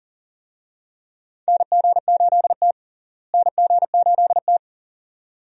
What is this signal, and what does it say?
Morse code, heard as a keyed tone, about 40 wpm, 700 Hz. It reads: NG8T NG8T
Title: USA Call Signs - 4 Characters - Repeated 2x - 40wpm